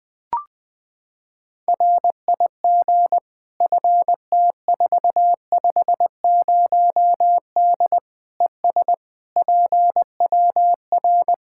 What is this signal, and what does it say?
Morse code, heard as a keyed tone, about 20 wpm, 700 Hz. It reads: RIG FT450D ES PWR
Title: Practice QSO - Exchange 2 - Info - 20wpm